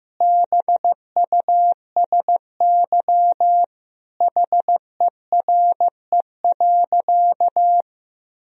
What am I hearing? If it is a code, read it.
Morse code, heard as a keyed tone, about 15 wpm, 700 Hz. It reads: BUSY HERE.
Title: Sentences from Top 1000 Words - Review Part 3 - 15wpm